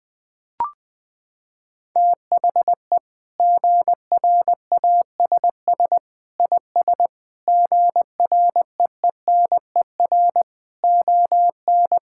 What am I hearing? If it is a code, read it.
Morse code, heard as a keyed tone, about 20 wpm, 700 Hz. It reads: THE GRASS IS GREENER ON
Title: Sentences with 4-Word Phrases - 20wpm